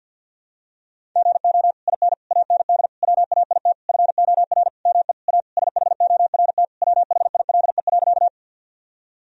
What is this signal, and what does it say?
Morse code, heard as a keyed tone, about 46 wpm, 700 Hz. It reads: GO IN AND WAIT FOR ME A SHORT WHILE.
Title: Sentences from Top 1000 Words - Review Part 2 - 50wpm